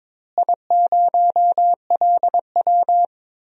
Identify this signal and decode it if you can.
Morse code, heard as a keyed tone, about 22 wpm, 700 Hz. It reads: I0LW
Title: DXCC-100 Call Signs - 4 Characters - Repeated 3x - 22wpm